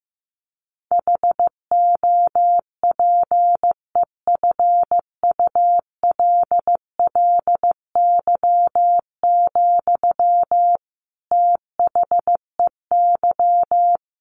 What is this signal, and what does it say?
Morse code, heard as a keyed tone, about 15 wpm, 700 Hz. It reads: HOPEFULLY, THEY
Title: Sentences from Top 1500 Words 15wpm